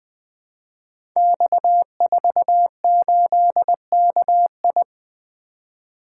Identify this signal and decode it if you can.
Morse code, heard as a keyed tone, about 20 wpm, 700 Hz. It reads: X48KI
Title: Code Groups - 5 Letter:Numbers - 20wpm